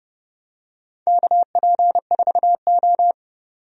Morse code, heard as a keyed tone, about 30 wpm, 700 Hz. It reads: KP4O
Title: DXCC-100 Call Signs - 4 Characters - 30wpm